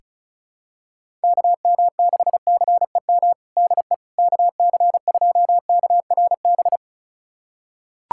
Morse code, heard as a keyed tone, about 34 wpm, 700 Hz. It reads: KM6CEM DE KC2KRB
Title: USA Callsign DE Callsign - 6 Characters - 35wpm